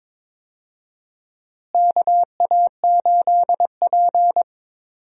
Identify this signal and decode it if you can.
Morse code, heard as a keyed tone, about 22 wpm, 700 Hz. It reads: KA8P
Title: USA Call Signs - 4 Characters - Repeated 2x - 22wpm